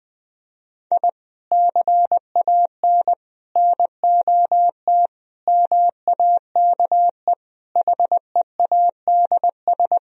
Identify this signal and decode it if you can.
Morse code, heard as a keyed tone, about 20 wpm, 700 Hz. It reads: I CAN NOT MAKE HEADS
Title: Sentences from Top 1000 Words - Review Part 3 - 20wpm